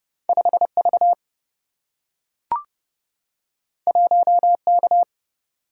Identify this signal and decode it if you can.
Morse code, heard as a keyed tone, about 30 wpm, 700 Hz. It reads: 5 V 1 K
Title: Single-Letter Single-Number 30wpm